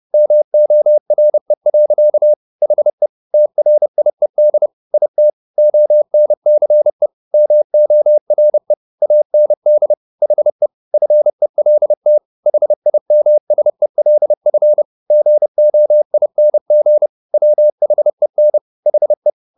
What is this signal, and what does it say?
Morse code, heard as a keyed tone, about 30 wpm, 600 Hz. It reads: MORE. HE TRIED IT ONCE MORE AND HE FELT HIMSELF GOING WHEN HE